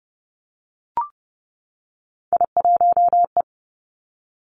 Morse code, heard as a keyed tone, about 30 wpm, 700 Hz. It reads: I1E
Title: Three Challenging Letter-Number Combinations 30wpm